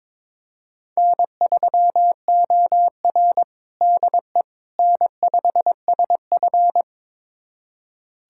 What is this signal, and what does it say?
Morse code, heard as a keyed tone, about 22 wpm, 700 Hz. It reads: N3OR DE N5SF
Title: USA Callsign DE Callsign - 4 Characters - 22wpm